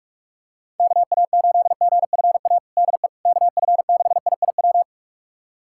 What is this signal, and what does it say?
Morse code, heard as a keyed tone, about 46 wpm, 700 Hz. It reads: KA8GFA DE KF6IIW
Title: USA Callsign DE Callsign 45wpm